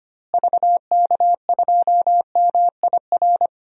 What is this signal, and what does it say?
Morse code, heard as a keyed tone, about 25 wpm, 700 Hz. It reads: VK2MIR
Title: DXCC-100 Call Signs - 6 Characters - Repeated 2x - 25wpm